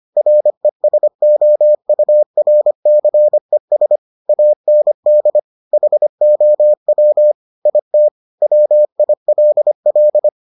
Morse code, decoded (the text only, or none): RESOURCES AND HOW IT WILL